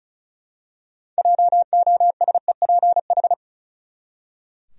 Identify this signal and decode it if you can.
Morse code, heard as a keyed tone, about 34 wpm, 700 Hz. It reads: JOSEPH
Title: USA Common Names - 35wpm